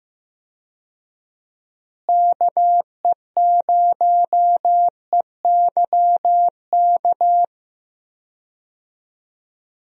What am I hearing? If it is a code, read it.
Morse code, heard as a keyed tone, about 15 wpm, 700 Hz. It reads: KE0EYK
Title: USA Call Signs - 6 Characters - Repeated 3x - 15wpm